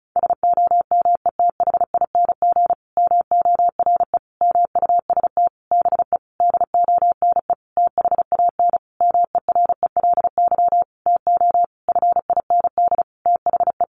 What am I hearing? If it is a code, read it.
Morse code, heard as a keyed tone, about 34 wpm, 700 Hz. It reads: SOMETHING MORE MUST BE DONE THAN MERELY TO FIND THE
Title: Sentences from Top 1400 Words 35wpm